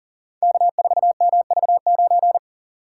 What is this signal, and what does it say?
Morse code, heard as a keyed tone, about 40 wpm, 700 Hz. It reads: K4MV9
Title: Code Groups - 5 Letter:Numbers - 40wpm